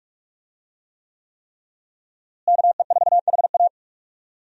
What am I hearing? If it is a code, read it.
Morse code, heard as a keyed tone, about 46 wpm, 700 Hz. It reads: KE4HA
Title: USA Call Signs - 5 Characters - Repeated 3x - 45wpm